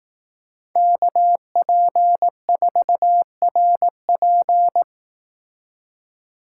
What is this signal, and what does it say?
Morse code, heard as a keyed tone, about 18 wpm, 700 Hz. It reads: KP4RP
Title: Speed Racing - USA 2x2 Call Signs 15wpm